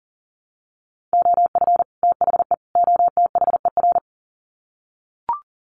Morse code, heard as a keyed tone, about 40 wpm, 700 Hz. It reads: OF THE OTHER
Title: Sets of 3 Words 40wpm